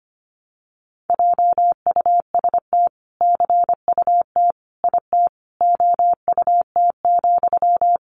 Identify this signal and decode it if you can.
Morse code, heard as a keyed tone, about 25 wpm, 700 Hz. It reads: JUST CUT IT OUT,
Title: Sentences from Top 300 Words 25wpm